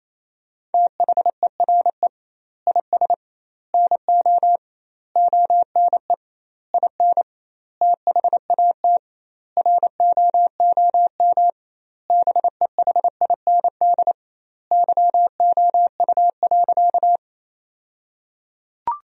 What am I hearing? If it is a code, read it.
Morse code, heard as a keyed tone, about 28 wpm, 700 Hz. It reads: THERE IS NO ONE IN THAT ROOM BEHIND YOU.
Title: 2x Word-Spacing - Sentences from Top 500 Words - 28wpm